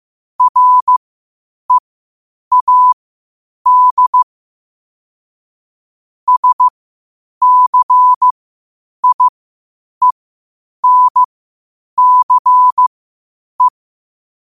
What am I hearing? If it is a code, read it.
Morse code, heard as a keyed tone, about 15 wpm, spaced as for 8 wpm, 1000 Hz. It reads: READ SCIENCE
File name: qotdfiles / 8wpm